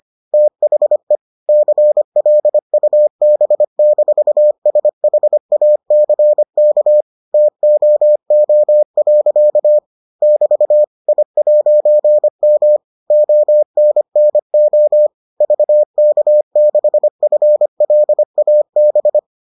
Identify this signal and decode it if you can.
Morse code, heard as a keyed tone, about 25 wpm, 600 Hz. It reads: THE CLUB-SHACK TOO. <BT> I'M ONNO VK6FLAB